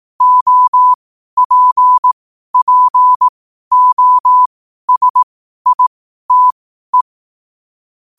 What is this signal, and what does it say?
Morse code, heard as a keyed tone, about 18 wpm, spaced as for 12 wpm, 1000 Hz. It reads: OPPOSITE
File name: qotdfiles / 12wpm